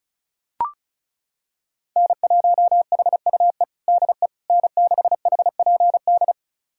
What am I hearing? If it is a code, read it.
Morse code, heard as a keyed tone, about 36 wpm, 700 Hz. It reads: N1HUE DE N6HPD
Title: USA Callsign DE Callsign 35wpm